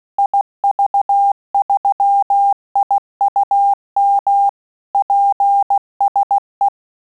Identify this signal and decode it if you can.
Morse code, heard as a keyed tone, about 16 wpm, 800 Hz. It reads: IV3IUM PSE
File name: QSO CW 75 CPM